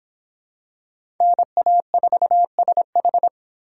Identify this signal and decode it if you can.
Morse code, heard as a keyed tone, about 26 wpm, 700 Hz. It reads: NA4SH